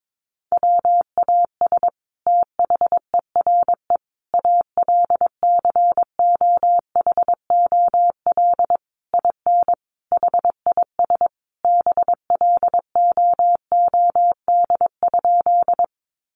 Morse code, heard as a keyed tone, about 22 wpm, 700 Hz. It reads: WAS THERE ALCOHOL IN HIS BLOOD?
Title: Sentences from Top 1400 Words 22wpm